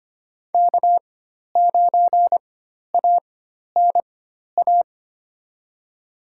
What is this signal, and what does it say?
Morse code, heard as a keyed tone, about 25 wpm, spaced as for 12 wpm, 700 Hz. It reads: K9ANA
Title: USA Call Signs - 5 Characters - 4x Character Spacing - 25wpm